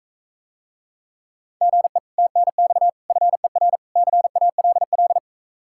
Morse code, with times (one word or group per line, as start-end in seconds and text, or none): 1.6-2.0 GE
2.2-2.9 TNX
3.1-3.8 FER
4.0-5.2 CALL